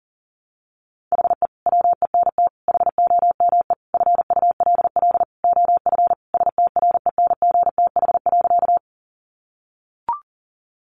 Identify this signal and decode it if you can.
Morse code, heard as a keyed tone, about 37 wpm, 700 Hz. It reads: HE WENT HOME FULL OF STRENGTH.
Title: Sentences from Top 1300 Words 40wpm